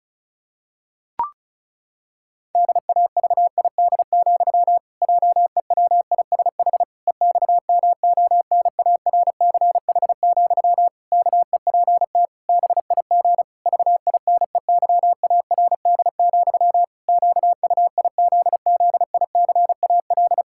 Morse code, heard as a keyed tone, about 34 wpm, 700 Hz. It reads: DAVID, JEWISH EXMONARCH, KEPT BIG VINEYARD, QUIZZICAL
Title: Pangram Challenge v2 - 35wpm